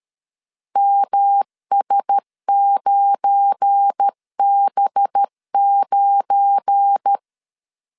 Morse code, heard as a keyed tone, about 13 wpm, 800 Hz. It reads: MS9B9
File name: LEZIONI / 7 GF9Z / 7 GF9 12wpm 2319